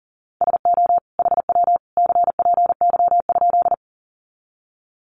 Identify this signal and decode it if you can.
Morse code, heard as a keyed tone, about 40 wpm, 700 Hz. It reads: SO HW CPY?
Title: Common QSO Phrases 40wpm